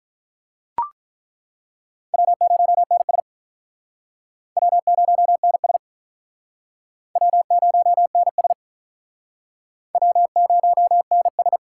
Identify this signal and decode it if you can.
Morse code, heard as a keyed tone, about 42 wpm, 700 Hz. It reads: W0NS W0NS W0NS W0NS
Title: Speed Racing - USA 1x2 Call Signs 35wpm